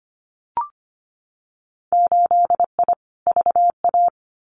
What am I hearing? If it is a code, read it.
Morse code, heard as a keyed tone, about 25 wpm, 700 Hz. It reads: 8I VA
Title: Winter Field Day Exchanges Encore 25wpm